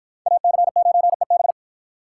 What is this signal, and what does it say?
Morse code, heard as a keyed tone, about 48 wpm, 700 Hz. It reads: AC9EB